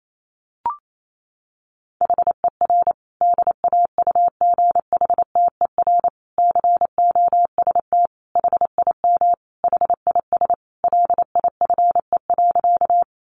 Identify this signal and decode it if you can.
Morse code, heard as a keyed tone, about 28 wpm, 700 Hz. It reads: HER DAUGHTER COST HIM HIS LIFE.
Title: Sentences from Top 1500 Words 28wpm